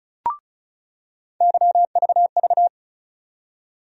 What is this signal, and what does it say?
Morse code, heard as a keyed tone, about 34 wpm, 700 Hz. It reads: YVV
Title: USA Call Sign Postfixes - 3 Letters 35wpm